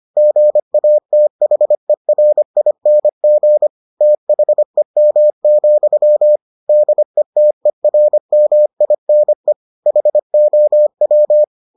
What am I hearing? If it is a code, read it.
Morse code, heard as a keyed tone, about 25 wpm, 600 Hz. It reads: GATHERING THEM, DETERMINE HOW